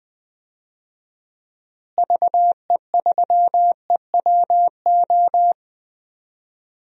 Morse code, heard as a keyed tone, about 21 wpm, 700 Hz. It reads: VE3EWO
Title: DXCC-100 Call Signs - 6 Characters - Repeated 3x - 20wpm